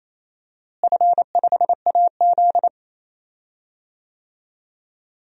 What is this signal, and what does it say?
Morse code, heard as a keyed tone, about 28 wpm, 700 Hz. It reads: F5AZ
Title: DXCC-100 Call Signs - 4 Characters - Repeated 2x - 28wpm